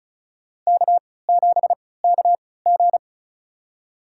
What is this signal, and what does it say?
Morse code, heard as a keyed tone, about 36 wpm, spaced as for 20 wpm, 700 Hz. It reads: K7KG
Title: USA Call Signs - 4 Characters - 3x Character Spacing - 35wpm